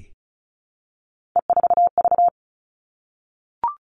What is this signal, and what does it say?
Morse code, heard as a keyed tone, about 34 wpm, 700 Hz. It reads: E4V
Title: All Cut Numbers - 3-Digit 35wpm